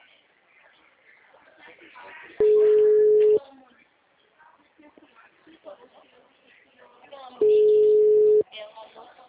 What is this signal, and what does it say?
Ringback tone (Europe, 425 Hz), with speech.